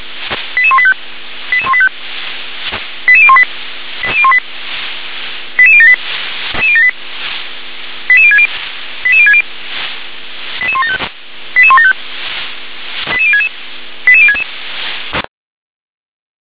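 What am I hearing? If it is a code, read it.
Five-tone selective call (ZVEI), heard as quick runs of five tones, tones of about 70 ms. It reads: 88185 88185 88018 88018 88078 88060 88060 88175 88175 88066 88066
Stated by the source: ZVEI five-tone selective call received on SDR.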